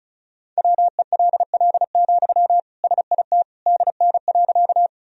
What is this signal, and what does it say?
Morse code, heard as a keyed tone, about 36 wpm, 700 Hz. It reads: WELL, SIT DN.
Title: Sentences from Top 500 Words and Abbreviations 35wpm